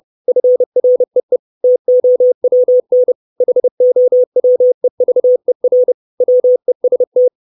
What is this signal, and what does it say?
Morse code, heard as a keyed tone, about 30 wpm, 500 Hz. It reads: FREE TOWN HOWEVER WEST
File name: 632 Words 30WPM